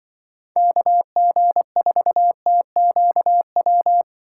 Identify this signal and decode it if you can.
Morse code, heard as a keyed tone, about 24 wpm, 700 Hz. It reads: KG4TQW
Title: Speed Racing - USA 2x3 Call Signs 20wpm